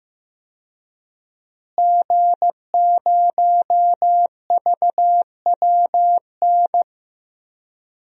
Morse code, heard as a keyed tone, about 15 wpm, 700 Hz. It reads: G0VWN
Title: DXCC-100 Call Signs - 5 Characters - Repeated 2x - 15wpm